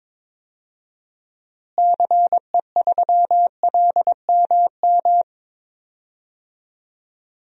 Morse code, heard as a keyed tone, about 22 wpm, 700 Hz. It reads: CE3LMM
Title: DXCC-100 Call Signs - 6 Characters - Repeated 3x - 22wpm